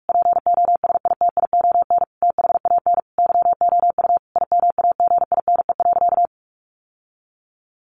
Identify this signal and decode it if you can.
Morse code, heard as a keyed tone, about 44 wpm, 700 Hz. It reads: POSITION THAN YOU IMAGINE.
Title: Sentences from Top 1400 Words 45wpm